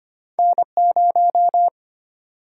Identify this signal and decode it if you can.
Morse code, heard as a keyed tone, about 35 wpm, spaced as for 30 wpm, 700 Hz. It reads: N0
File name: practice / 25wpm / Speed-Racing - Two Letter-Number Combinations - 25wpm